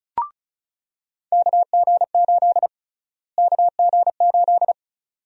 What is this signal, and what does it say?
Morse code, heard as a keyed tone, about 36 wpm, 700 Hz. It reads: KG8 KG8
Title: USA Call Sign Prefixes - Repeated 2x - 35wpm